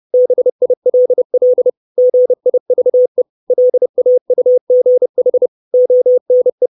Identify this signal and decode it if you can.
Morse code, heard as a keyed tone, about 30 wpm, 500 Hz. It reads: BILL GIVE LAUGH ONE
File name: 632 words / 632 Words 30WPM